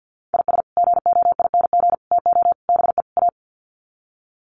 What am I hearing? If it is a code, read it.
Morse code, heard as a keyed tone, about 50 wpm, 700 Hz. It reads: IS GOING TO BE A
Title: Sets of 5 Words 50wpm